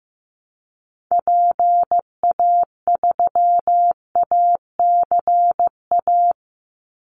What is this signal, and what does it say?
Morse code, heard as a keyed tone, about 15 wpm, 700 Hz. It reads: PA3ACA
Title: Call Signs Repeated 3x - 15wpm